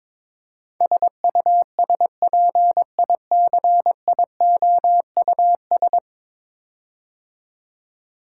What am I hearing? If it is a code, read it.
Morse code, heard as a keyed tone, about 22 wpm, 700 Hz. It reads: SUSPICIOUS
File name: practice / 22wpm / 10 Letter Words - First 3 Letters Given - 22wpm